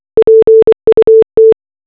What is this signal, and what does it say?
Morse code, heard as a keyed tone, about 24 wpm, 440 Hz. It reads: PUT
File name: RMS Titanic distress signal simulated as morse code